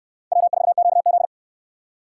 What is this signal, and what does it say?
Morse code, heard as a keyed tone, about 69 wpm, 700 Hz. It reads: W4QZ